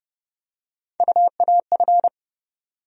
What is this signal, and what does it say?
Morse code, heard as a keyed tone, about 30 wpm, 700 Hz. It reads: UAF
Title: USA Call Sign Postfixes - 3 Letters 30wpm